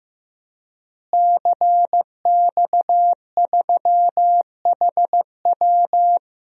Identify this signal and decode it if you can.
Morse code, heard as a keyed tone, about 15 wpm, 700 Hz. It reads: CX3HW
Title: DXCC-100 Call Signs - 5 Characters - Repeated 3x - 15wpm